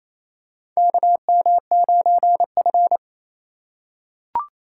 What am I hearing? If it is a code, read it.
Morse code, heard as a keyed tone, about 28 wpm, 700 Hz. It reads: KM9F
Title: USA Call Signs - 4 Characters - 3x Character Spacing - 28wpm